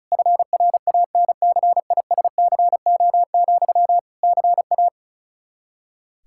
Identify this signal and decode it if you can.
Morse code, heard as a keyed tone, about 34 wpm, 700 Hz. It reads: FRANCISCO, CA
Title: Practice QSO - Exchange 1 - Beginning - 35wpm